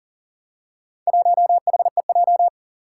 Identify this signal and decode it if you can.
Morse code, heard as a keyed tone, about 40 wpm, 700 Hz. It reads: 1HEJ
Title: Four Challenging Letter-Number Combinations 40wpm